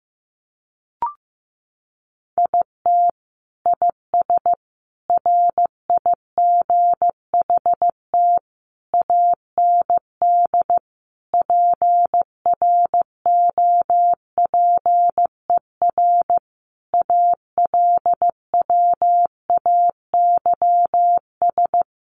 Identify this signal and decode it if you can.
Morse code, heard as a keyed tone, about 15 wpm, 700 Hz. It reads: IT IS RIGHT AND PROPER ALWAYS
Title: Sentences from Top 1500 Words 15wpm